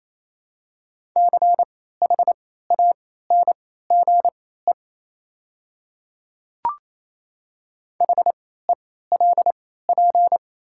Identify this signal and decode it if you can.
Morse code, heard as a keyed tone, about 28 wpm, spaced as for 16 wpm, 700 Hz. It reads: CHANGE HELP
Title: 3x Character Spacing - Top 200 Words - 28wpm